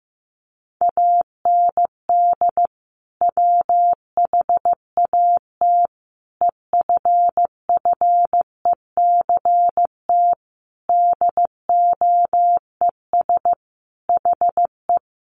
Morse code, heard as a keyed tone, about 15 wpm, 700 Hz. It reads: AND WHAT EFFECT DOES HE
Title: Sentences from Top 1000 Words 15wpm